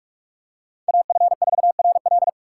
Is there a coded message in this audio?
Yes: Morse code from 0.9 to 2.3 s, heard as a keyed tone, about 48 wpm, 700 Hz. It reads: AF4RL